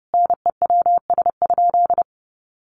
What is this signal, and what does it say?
Morse code, heard as a keyed tone, about 30 wpm, 700 Hz. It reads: NEWS?
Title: Sentences from Top 800 Words 30wpm